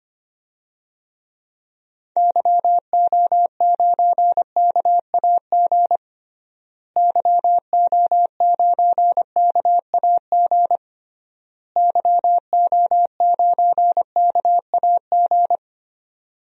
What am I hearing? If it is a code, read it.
Morse code, heard as a keyed tone, about 25 wpm, 700 Hz. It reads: YO9KAG YO9KAG YO9KAG
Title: DXCC-100 Call Signs - 6 Characters - Repeated 3x - 25wpm